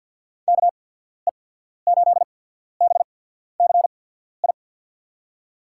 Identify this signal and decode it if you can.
Morse code, heard as a keyed tone, about 49 wpm, spaced as for 14 wpm, 700 Hz. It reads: KE8BCI
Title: USA Call Signs - 6 Characters - 8x Character Spacing - 50wpm